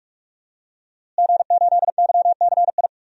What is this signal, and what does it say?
Morse code, heard as a keyed tone, about 46 wpm, 700 Hz. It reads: G8YCI
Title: DXCC-100 Call Signs - 5 Characters - Repeated 2x - 45wpm